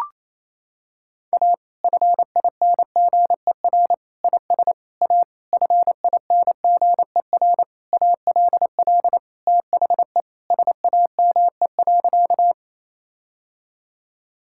Morse code, heard as a keyed tone, about 28 wpm, 700 Hz. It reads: A FINGER IS A FINGER ALL THE SAME.
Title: Sentences from Top 1600 Words 28wpm